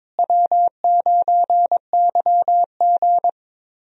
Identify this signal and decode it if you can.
Morse code, heard as a keyed tone, about 22 wpm, 700 Hz. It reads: W9YG
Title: USA Call Signs - 4 Characters - Repeated 2x - 22wpm